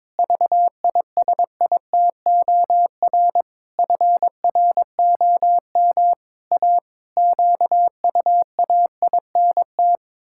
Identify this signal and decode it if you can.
Morse code, heard as a keyed tone, about 22 wpm, 700 Hz. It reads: VISITOR FROM A QUAINT
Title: Pangram Challenge v2 - 22wpm